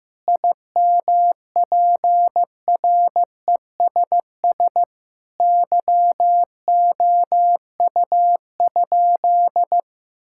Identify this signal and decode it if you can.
Morse code, heard as a keyed tone, about 15 wpm, 700 Hz. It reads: IMPRESS YOU?
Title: Sentences from Top 1600 Words 15wpm